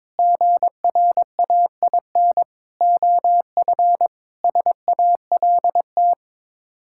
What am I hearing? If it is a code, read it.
Morse code, heard as a keyed tone, about 22 wpm, 700 Hz. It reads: GRAIN OF SALT
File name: practice / 22wpm / Sets of 7 Words - 22wpm